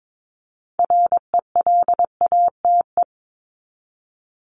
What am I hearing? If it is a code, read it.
Morse code, heard as a keyed tone, about 22 wpm, 700 Hz. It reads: RELATE
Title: ING Suffix - Top 100 Words with Base Words 22wpm